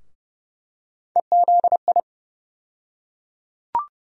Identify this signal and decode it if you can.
Morse code, heard as a keyed tone, about 30 wpm, 700 Hz. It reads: EZI
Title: USA Call Sign Postfixes - 3 Letters 30wpm